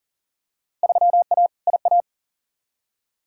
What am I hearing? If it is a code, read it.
Morse code, heard as a keyed tone, about 40 wpm, 700 Hz. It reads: 3A IA
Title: Field Day Exchanges Encore - 40wpm